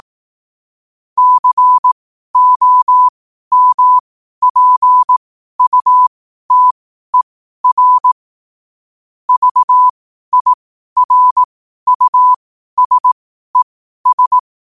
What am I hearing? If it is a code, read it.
Morse code, heard as a keyed tone, about 18 wpm, spaced as for 12 wpm, 1000 Hz. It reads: COMPUTER VIRUSES
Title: 12 WPM morse code quotes for Fri, 01 Mar 2024